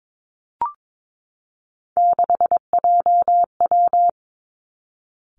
Morse code, heard as a keyed tone, about 22 wpm, 700 Hz. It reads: 6JW
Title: Three Challenging Letter-Number Combinations 22wpm